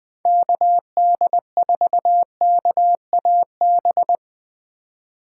Morse code, heard as a keyed tone, about 20 wpm, 700 Hz. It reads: KD4KAB
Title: USA Call Signs - 6 Characters - 4x Character Spacing - 20wpm